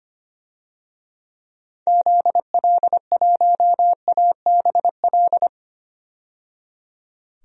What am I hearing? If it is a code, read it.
Morse code, heard as a keyed tone, about 25 wpm, 700 Hz. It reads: ZL1ABL
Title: DXCC-100 Call Signs - 6 Characters - 25wpm